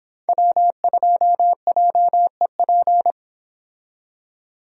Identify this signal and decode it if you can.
Morse code, heard as a keyed tone, about 26 wpm, 700 Hz. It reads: W2JEP